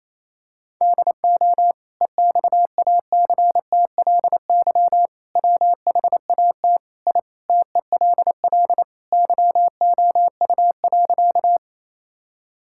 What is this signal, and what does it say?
Morse code, heard as a keyed tone, about 28 wpm, 700 Hz. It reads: DO EXACTLY WHAT I TELL YOU.
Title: Sentences from Top 1000 Words - Review Part 6 - 28wpm